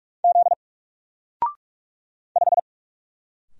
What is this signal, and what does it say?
Morse code, heard as a keyed tone, about 44 wpm, 700 Hz. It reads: Z 5
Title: Single Character (Letter:Number:Punctuation:Prosign) 45wpm